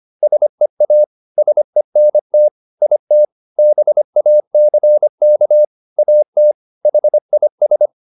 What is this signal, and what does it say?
Morse code, heard as a keyed tone, about 25 wpm, 600 Hz. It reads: SEA SENT IT BACK AT HIS
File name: the old man and the sea part 1 25wpm